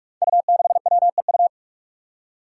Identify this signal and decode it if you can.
Morse code, heard as a keyed tone, about 46 wpm, 700 Hz. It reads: U6WEU
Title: Five Challenging Letter-Number Combinations 45wpm